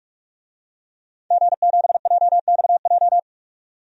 Morse code, heard as a keyed tone, about 45 wpm, 700 Hz. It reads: G7JXJ
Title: DXCC-100 Call Signs - 5 Characters - 45wpm